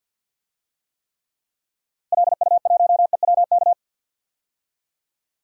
Morse code, heard as a keyed tone, about 51 wpm, 700 Hz. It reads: LU1EWK
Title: DXCC-100 Call Signs - 6 Characters - Repeated 3x - 50wpm